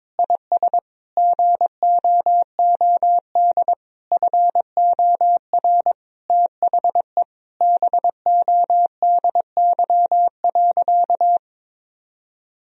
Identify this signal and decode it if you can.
Morse code, heard as a keyed tone, about 22 wpm, 700 Hz. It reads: IS GOOD FOR THE BODY.
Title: Sentences from Top 1000 Words - Review Part 4 - 22wpm